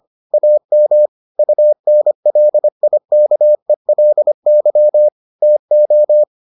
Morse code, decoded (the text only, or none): AM UNLIKELY TO